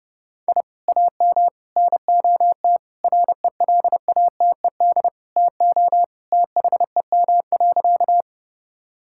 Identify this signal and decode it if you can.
Morse code, heard as a keyed tone, about 30 wpm, 700 Hz. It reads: I AM NOT RELATED TO THEM.
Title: Sentences from Top 1000 Words - Review Part 5 - 30wpm